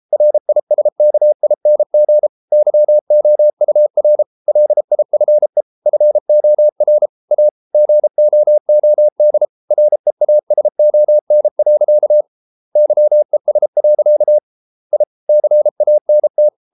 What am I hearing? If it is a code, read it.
Morse code, heard as a keyed tone, about 33 wpm, 600 Hz. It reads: RISKING YOUR LIFE FOR A GOOD REASON. YES. I CANT